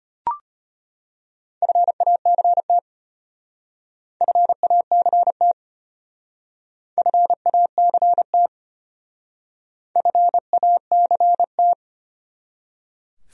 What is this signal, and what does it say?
Morse code, heard as a keyed tone, about 32 wpm, 700 Hz. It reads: FACT FACT FACT FACT